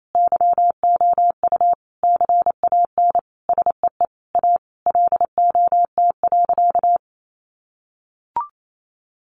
Morse code, heard as a keyed tone, about 28 wpm, 700 Hz. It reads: YOU CAN SEE A LOT.
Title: Sentences from Top 1500 Words 28wpm